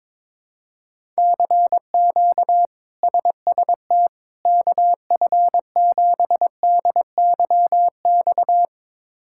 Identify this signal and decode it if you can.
Morse code, heard as a keyed tone, about 22 wpm, 700 Hz. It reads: CQ SST KF7DYX
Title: K1USN SST QSO Practice - 22wpm